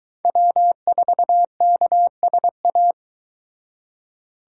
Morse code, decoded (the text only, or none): W4KSA